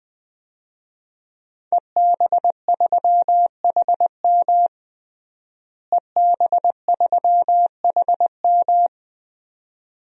Morse code, heard as a keyed tone, about 21 wpm, 700 Hz. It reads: EB3HM EB3HM
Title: DXCC-100 Call Signs - 5 Characters - Repeated 2x - 20wpm